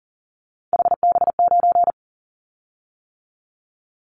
Morse code, heard as a keyed tone, about 40 wpm, 700 Hz. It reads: HB9
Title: Callsign Prefixes for Top 25 DXCC - 40wpm